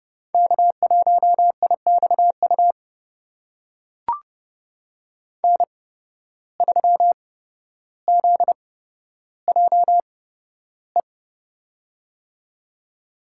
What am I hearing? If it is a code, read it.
Morse code, heard as a keyed tone, about 29 wpm, spaced as for 13 wpm, 700 Hz. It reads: K1IXU N3ZJE
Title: USA Call Signs - 5 Characters - 8x Character Spacing - 30wpm